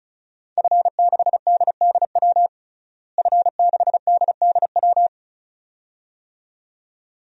Morse code, heard as a keyed tone, about 36 wpm, 700 Hz. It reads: F6DDW F6DDW
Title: DXCC-100 Call Signs - 5 Characters - Repeated 2x - 35wpm